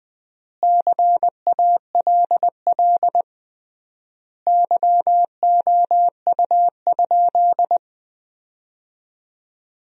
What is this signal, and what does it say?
Morse code, heard as a keyed tone, about 20 wpm, 700 Hz. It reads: CALL YOU?
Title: 3x Word-Spacing - Sentences from Top 100 Words - 20wpm